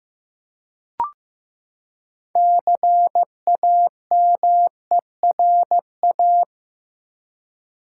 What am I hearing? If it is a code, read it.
Morse code, heard as a keyed tone, about 15 wpm, 700 Hz. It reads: CAMERA
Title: Six Letter Words - 15wpm